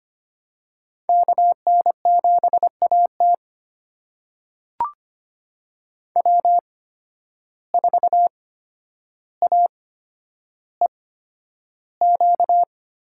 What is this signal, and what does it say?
Morse code, heard as a keyed tone, about 25 wpm, spaced as for 11 wpm, 700 Hz. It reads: KN7AT W4AEQ
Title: USA Call Signs - 5 Characters - 8x Character Spacing - 25wpm